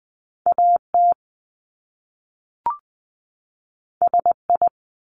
Morse code, heard as a keyed tone, about 20 wpm, 700 Hz. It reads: AT SI
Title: Top Two Letter Combinations in Words 20wpm